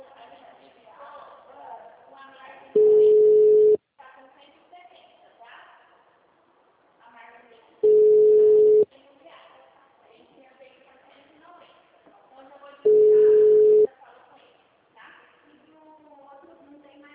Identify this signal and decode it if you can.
Ringback tone (Europe, 425 Hz), with speech.
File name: out- -2001- - - .8316